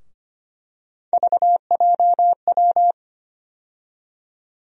Morse code, heard as a keyed tone, about 25 wpm, 700 Hz. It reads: VJW
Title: USA Call Sign Postfixes - 3 Letters 25wpm